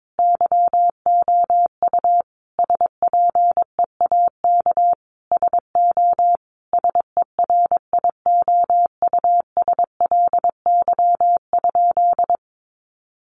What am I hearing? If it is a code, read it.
Morse code, heard as a keyed tone, about 22 wpm, 700 Hz. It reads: YOU SPEAK SO SERIOUSLY?
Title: Sentences from Top 1200 Words 22wpm